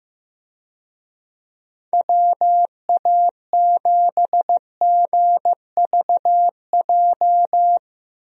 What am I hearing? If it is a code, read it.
Morse code, heard as a keyed tone, about 15 wpm, 700 Hz. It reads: WA7GVJ
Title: USA Call Signs - 6 Characters - 15wpm